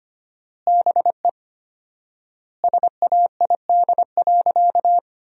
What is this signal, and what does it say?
Morse code, heard as a keyed tone, about 25 wpm, 700 Hz. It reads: BE SAID.
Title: 4x Word-Spacing - Sentences from Top 500 Words - 25wpm